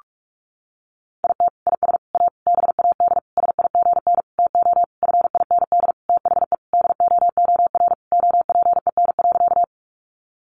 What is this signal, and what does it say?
Morse code, heard as a keyed tone, about 44 wpm, 700 Hz. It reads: IT IS A BAD SIGN TO FIND THE DOOR OPEN.
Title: Sentences from Top 1000 Words 45wpm